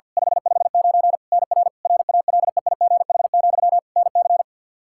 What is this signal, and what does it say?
Morse code, heard as a keyed tone, about 52 wpm, 700 Hz. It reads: 559 NR RALEIGH, NC
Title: Practice QSO - Exchange 1 - Beginning - 50wpm